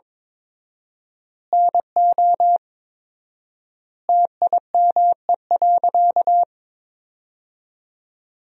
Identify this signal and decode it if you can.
Morse code, heard as a keyed tone, about 22 wpm, 700 Hz. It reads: NO TIME.
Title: 4x Word-Spacing - Sentences from Top 100 Words - 22wpm